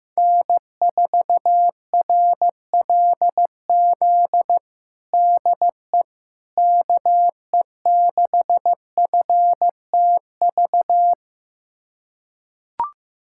Morse code, heard as a keyed tone, about 15 wpm, 700 Hz. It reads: N4RLZ DE KE6FTV
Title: USA Callsign DE Callsign 15wpm